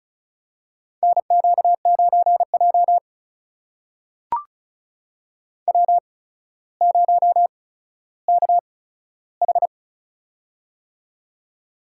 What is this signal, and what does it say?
Morse code, heard as a keyed tone, about 36 wpm, spaced as for 15 wpm, 700 Hz. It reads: NQ9J W0KH
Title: USA Call Signs - 4 Characters - 8x Character Spacing - 35wpm